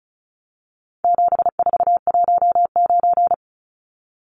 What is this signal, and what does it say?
Morse code, heard as a keyed tone, about 34 wpm, 700 Hz. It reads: 7419
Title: Numbers - 4-Digit 35wpm